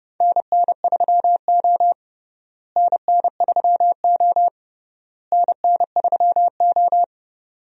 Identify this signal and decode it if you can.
Morse code, heard as a keyed tone, about 30 wpm, 700 Hz. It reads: NN3O NN3O NN3O
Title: USA Call Signs - 4 Characters - Repeated 3x - 30wpm